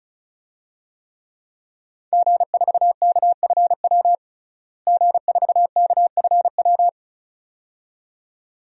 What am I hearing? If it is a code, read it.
Morse code, heard as a keyed tone, about 36 wpm, 700 Hz. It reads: G4KFW G4KFW
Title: DXCC-100 Call Signs - 5 Characters - Repeated 2x - 35wpm